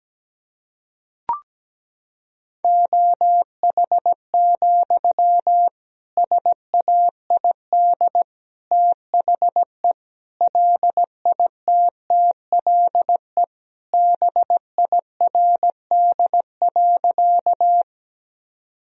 Morse code, heard as a keyed tone, about 17 wpm, 700 Hz. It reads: OH, SAID THE LITTLE BIRD.
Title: Sentences from Top 1000 Words - Review Part 2 - 17wpm